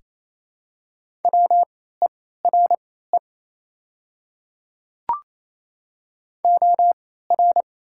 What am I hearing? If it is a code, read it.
Morse code, heard as a keyed tone, about 28 wpm, spaced as for 16 wpm, 700 Hz. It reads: WERE OR
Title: 3x Character Spacing - Top 100 Words - 28wpm